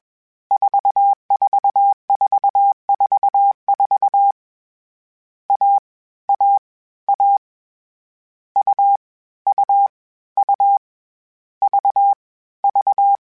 Morse code, heard as a keyed tone, about 21 wpm, 800 Hz. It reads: 44444 A A A U U U V V
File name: Difficult Combos / Difficult Combos-mp3s / AUV AUV4 @20